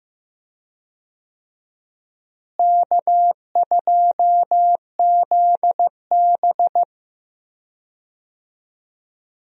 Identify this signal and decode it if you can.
Morse code, heard as a keyed tone, about 16 wpm, 700 Hz. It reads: K2ZB
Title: USA Call Signs - 4 Characters - Repeated 3x - 15wpm